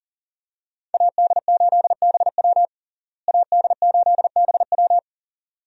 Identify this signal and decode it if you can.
Morse code, heard as a keyed tone, about 37 wpm, 700 Hz. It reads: AD8BW AD8BW
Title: USA Call Signs - 5 Characters - Repeated 2x - 40wpm